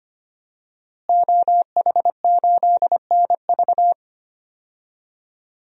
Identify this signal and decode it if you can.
Morse code, heard as a keyed tone, about 25 wpm, 700 Hz. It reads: OH8NV
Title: DXCC-100 Call Signs - 5 Characters - Repeated 3x - 25wpm